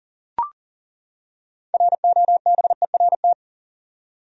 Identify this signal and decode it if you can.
Morse code, heard as a keyed tone, about 40 wpm, 700 Hz. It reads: ROBERT
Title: USA Common Names - 40wpm